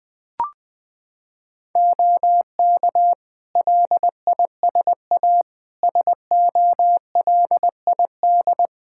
Morse code, heard as a keyed tone, about 20 wpm, 700 Hz. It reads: OK LISA SOLID
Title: Practice QSO - Exchange 3 - More Info - 20wpm